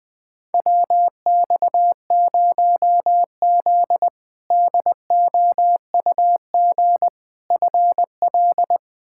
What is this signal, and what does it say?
Morse code, heard as a keyed tone, about 20 wpm, 700 Hz. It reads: WX0Z DOUG FL
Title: K1USN SST QSO Practice - 20wpm